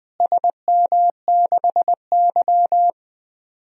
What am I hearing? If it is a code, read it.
Morse code, heard as a keyed tone, about 20 wpm, 700 Hz. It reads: SM6Y
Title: DXCC-100 Call Signs - 4 Characters - Repeated 3x - 20wpm